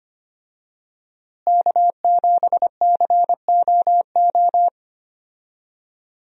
Morse code, heard as a keyed tone, about 25 wpm, 700 Hz. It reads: K7COO
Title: USA Call Signs - 5 Characters - Repeated 3x - 25wpm